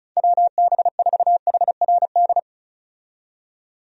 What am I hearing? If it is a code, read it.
Morse code, heard as a keyed tone, about 36 wpm, 700 Hz. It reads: WB4HRD
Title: USA Call Signs - 6 Characters - 35wpm